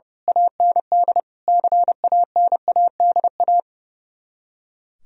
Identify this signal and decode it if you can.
Morse code, heard as a keyed tone, about 28 wpm, 700 Hz. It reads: AND CANADA
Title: Speed-Racing - Sets of 5 Words 30wpm